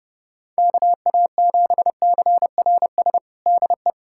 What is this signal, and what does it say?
Morse code, heard as a keyed tone, about 31 wpm, 700 Hz. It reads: KA7CRS DE
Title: USA Callsign DE Callsign 30wpm